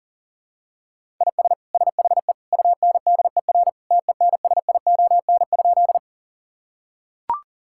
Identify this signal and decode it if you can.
Morse code, heard as a keyed tone, about 40 wpm, 700 Hz. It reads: IS SHE UNDER TENSION?
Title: Sentences from Top 1600 Words 40wpm